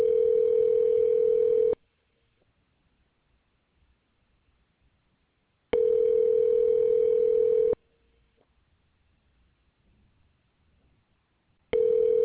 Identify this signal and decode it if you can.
Ringback tone (North America), 440 + 480 Hz.